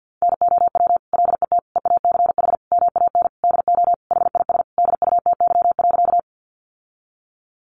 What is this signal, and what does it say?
Morse code, heard as a keyed tone, about 50 wpm, 700 Hz. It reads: NOW LET EACH MAN DO HIS DUTY.
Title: Sentences from Top 1500 Words 50wpm